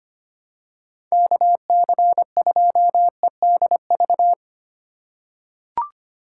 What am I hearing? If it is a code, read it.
Morse code, heard as a keyed tone, about 25 wpm, 700 Hz. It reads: KC2EDV
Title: USA Call Signs - 6 Characters - 4x Character Spacing - 25wpm